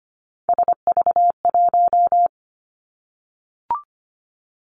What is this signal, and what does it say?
Morse code, heard as a keyed tone, about 25 wpm, 700 Hz. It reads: SV1
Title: Callsign Prefixes for Top 50 DXCC - 25wpm